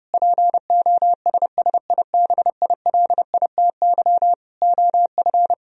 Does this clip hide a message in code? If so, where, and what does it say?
Yes: Morse code from 0.1 to 5.5 s, heard as a keyed tone, about 30 wpm, 700 Hz. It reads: POSSIBILITY OF